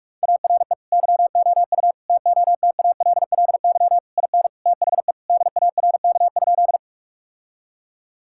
Morse code, heard as a keyed tone, about 42 wpm, 700 Hz. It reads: ARE YOU TOTALLY IN THE DARK?
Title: Sentences from Top 1600 Words 45wpm